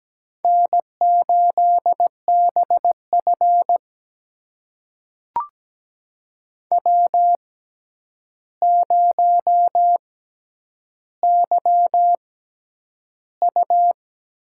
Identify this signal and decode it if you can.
Morse code, heard as a keyed tone, about 17 wpm, spaced as for 9 wpm, 700 Hz. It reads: N8BF W0YU
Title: USA Call Signs - 4 Characters - 6x Character Spacing - 17wpm